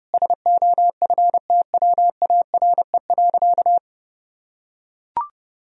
Morse code, heard as a keyed tone, about 32 wpm, 700 Hz. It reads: SOFTWARE.
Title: Sentences from Top 1000 Words - Review Part 3 - 30wpm